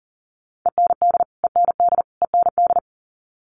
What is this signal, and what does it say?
Morse code, heard as a keyed tone, about 40 wpm, 700 Hz. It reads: END END END
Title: Top 200 Words Repeated 40wpm